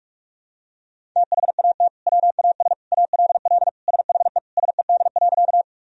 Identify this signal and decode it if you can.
Morse code, heard as a keyed tone, about 41 wpm, 700 Hz. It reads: THAT WAS ALL SHE SED.
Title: Sentences from Top 100 Words and Abbreviations 45wpm